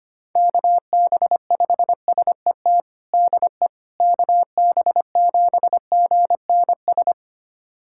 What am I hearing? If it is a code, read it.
Morse code, heard as a keyed tone, about 25 wpm, 700 Hz. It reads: KB5SET DE KB7GNS
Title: USA Callsign DE Callsign - 6 Characters - 25wpm